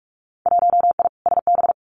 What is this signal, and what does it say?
Morse code, heard as a keyed tone, about 46 wpm, 700 Hz. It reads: 1I SB
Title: Winter Field Day Exchanges Encore 45wpm